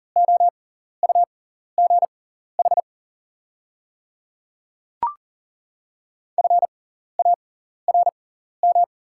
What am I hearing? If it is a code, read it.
Morse code, heard as a keyed tone, about 40 wpm, spaced as for 14 wpm, 700 Hz. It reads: OUGH FARM
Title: 6x Character Spacing - Top 500 Words - 40wpm